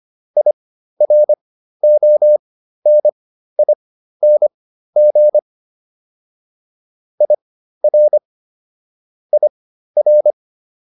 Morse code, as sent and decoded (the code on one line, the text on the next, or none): .. .-. --- -. .. -. --. / .. .-. / .. .-.
IRONING IR IR